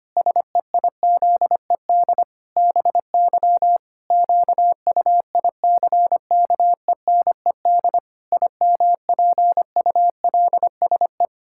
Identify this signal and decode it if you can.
Morse code, heard as a keyed tone, about 25 wpm, 700 Hz. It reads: SEIZED BY QUICKENED IMPULSE
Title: Pangram Challenge v2 - 25wpm